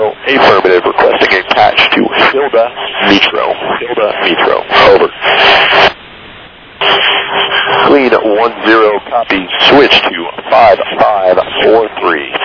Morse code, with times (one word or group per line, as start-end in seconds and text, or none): none